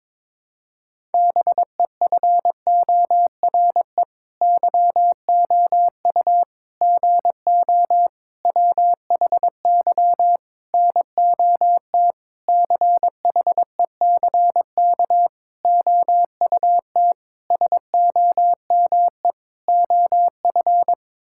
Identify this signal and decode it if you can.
Morse code, heard as a keyed tone, about 22 wpm, 700 Hz. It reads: BEFORE YOU GO WHY NOT CHECK OUT SOME OF
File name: Sets of 9 Words - 22wpm